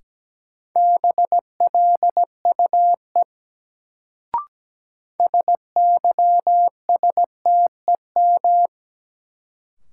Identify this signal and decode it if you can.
Morse code, heard as a keyed tone, about 17 wpm, 700 Hz. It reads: BLUE SYSTEM
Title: Sentences from Top 1000 Words - Review Part 3 - New Word Drill - 17wpm